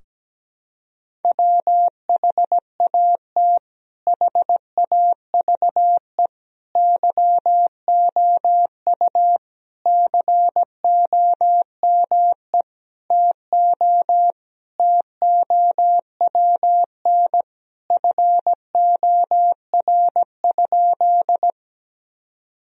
Morse code, heard as a keyed tone, about 17 wpm, 700 Hz. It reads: WHAT HAVE YOU COME TO TOWN FOR?
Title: Sentences from Top 1000 Words - Review Part 2 - 17wpm